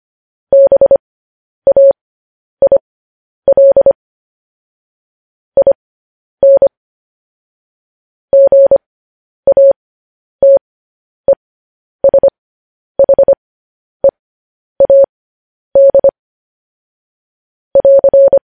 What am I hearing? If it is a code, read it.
Morse code, heard as a keyed tone, about 25 wpm, spaced as for 10 wpm, 550 Hz. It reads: BAIL IN GATESHEAD <AR>
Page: news-headlines-in-morse-code-at-10-wpm